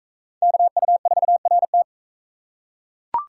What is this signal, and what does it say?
Morse code, heard as a keyed tone, about 36 wpm, 700 Hz. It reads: KU4RT